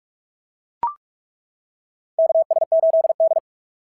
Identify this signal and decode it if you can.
Morse code, heard as a keyed tone, about 42 wpm, 690 Hz. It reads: KS8D
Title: SOTA QSO Practice 45wpm